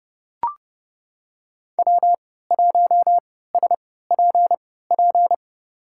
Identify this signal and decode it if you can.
Morse code, heard as a keyed tone, about 30 wpm, spaced as for 17 wpm, 700 Hz. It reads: W1SPP
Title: USA Call Signs - 5 Characters - 3x Character Spacing - 30wpm